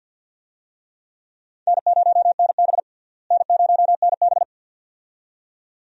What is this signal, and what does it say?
Morse code, heard as a keyed tone, about 50 wpm, 700 Hz. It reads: N0NB N0NB
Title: USA Call Signs - 4 Characters - Repeated 2x - 50wpm